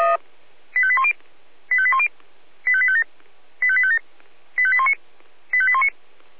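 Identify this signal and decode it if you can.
Five-tone selective call (ZVEI), heard as quick runs of five tones, tones of about 70 ms. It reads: 85810 85810 85857 85857 85819 85819
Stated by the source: ZVEI five-tone call 85820 received on SDR.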